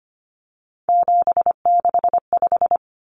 Morse code, heard as a keyed tone, about 25 wpm, 700 Hz. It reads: 765
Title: Numbers - 3-Digit 25wpm